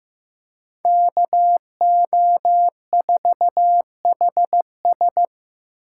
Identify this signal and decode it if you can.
Morse code, heard as a keyed tone, about 18 wpm, 700 Hz. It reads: KO4HS
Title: Speed Racing - USA Call Signs 15wpm